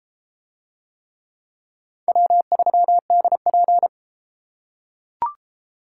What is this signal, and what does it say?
Morse code, heard as a keyed tone, about 29 wpm, 700 Hz. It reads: W3DP